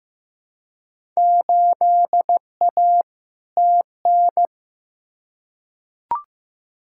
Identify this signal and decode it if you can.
Morse code, heard as a keyed tone, about 15 wpm, 700 Hz. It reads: 8A TN
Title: Field Day Exchanges Encore - 15wpm